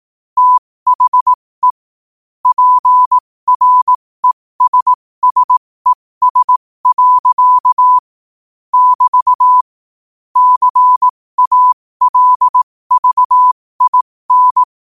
Morse code, heard as a keyed tone, about 18 wpm, spaced as for 15 wpm, 1000 Hz. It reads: THE PRESSES. <BT> CALVIN
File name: qotdfiles / 15wpm